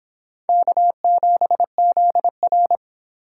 Morse code, heard as a keyed tone, about 24 wpm, 700 Hz. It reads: K7ZR